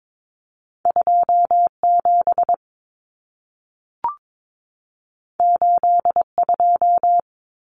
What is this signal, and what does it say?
Morse code, heard as a keyed tone, about 22 wpm, 700 Hz. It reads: 27 82
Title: Numbers - 2-Digit 22wpm